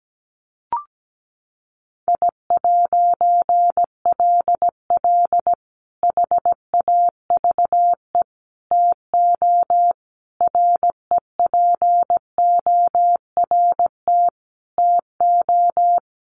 Morse code, heard as a keyed tone, about 17 wpm, 700 Hz. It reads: I'LL HAVE TO REPORT TO
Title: A Sound Of Thunder 17wpm